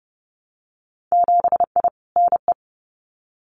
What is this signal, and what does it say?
Morse code, heard as a keyed tone, about 30 wpm, 700 Hz. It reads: 7I NE
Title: Winter Field Day Exchanges Encore 30wpm